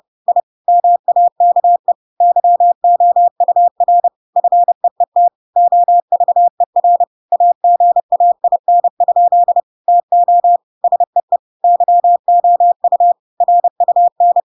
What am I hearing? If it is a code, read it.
Morse code, heard as a keyed tone, about 30 wpm, 700 Hz. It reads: I MAKE YOUR FEET OVER AGAIN? TO SEE YOU RUN
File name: Pinocchio / Pinocchio ch8 30 30 ik1hge